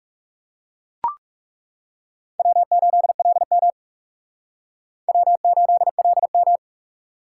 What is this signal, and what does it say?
Morse code, heard as a keyed tone, about 37 wpm, 700 Hz. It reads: W8LM W8LM
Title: Speed Racing - USA 1x2 Call Signs 30wpm